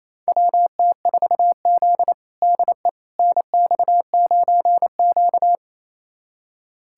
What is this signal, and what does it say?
Morse code, heard as a keyed tone, about 28 wpm, 700 Hz. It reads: WT4Z DE NX9Q
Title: USA Callsign DE Callsign - 4 Characters - 28wpm